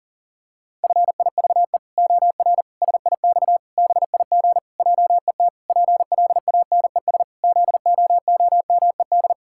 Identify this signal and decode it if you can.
Morse code, heard as a keyed tone, about 40 wpm, 700 Hz. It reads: FIVE OR SIX BIG JET PLANES ZOOMED
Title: Pangram Challenge v2 - 40wpm